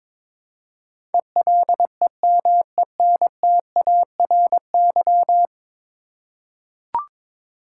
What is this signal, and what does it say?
Morse code, heard as a keyed tone, about 22 wpm, 700 Hz. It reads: ELEMENTARY
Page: morse-code-ninja-practice-22wpm